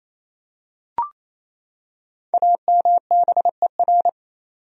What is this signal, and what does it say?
Morse code, heard as a keyed tone, about 28 wpm, 700 Hz. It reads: AMBER
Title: USA Common Names - Max 5 Letters - 28wpm